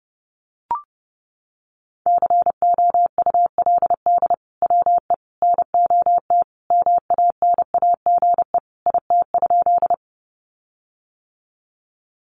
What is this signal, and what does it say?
Morse code, heard as a keyed tone, about 29 wpm, 700 Hz. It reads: COULD WE NOT MANAGE IT?
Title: Sentences from Top 1400 Words 30wpm